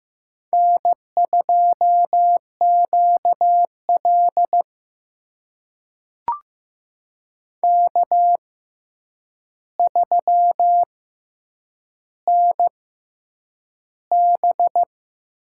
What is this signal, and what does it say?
Morse code, heard as a keyed tone, about 15 wpm, spaced as for 8 wpm, 700 Hz. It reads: N2QL K3NB
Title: USA Call Signs - 4 Characters - 6x Character Spacing - 15wpm